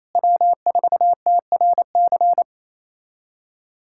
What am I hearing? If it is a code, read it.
Morse code, heard as a keyed tone, about 27 wpm, 700 Hz. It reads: W4TRC
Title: USA Callsign DE Callsign - 5 Characters - 28wpm